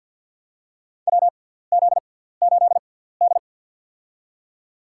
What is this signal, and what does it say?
Morse code, heard as a keyed tone, about 53 wpm, spaced as for 17 wpm, 700 Hz. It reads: WZ8D
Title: USA Call Signs - 4 Characters - 6x Character Spacing - 50wpm